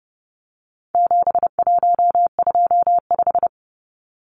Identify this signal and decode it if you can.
Morse code, heard as a keyed tone, about 30 wpm, 700 Hz. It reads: 7125
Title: Numbers - 4-Digit 30wpm